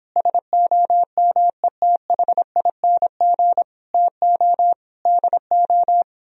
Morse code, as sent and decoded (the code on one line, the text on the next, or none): ... --- -- . - .... .. -. --. / - --- / -.. ---
SOMETHING TO DO